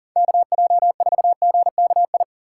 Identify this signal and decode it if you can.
Morse code, heard as a keyed tone, about 40 wpm, 700 Hz. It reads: KJ4GKI
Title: USA Call Signs - 6 Characters - 4x Character Spacing - 40wpm